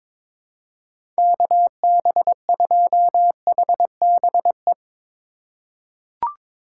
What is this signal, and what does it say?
Morse code, heard as a keyed tone, about 22 wpm, 700 Hz. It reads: KB2HBE
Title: USA Call Signs - 6 Characters - 2x Character Spacing - 22wpm